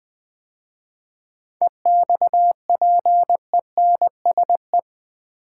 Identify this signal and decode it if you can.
Morse code, heard as a keyed tone, about 20 wpm, 700 Hz. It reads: EXPENSE
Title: 7 Letter Words - First 3 Letters Given - 20wpm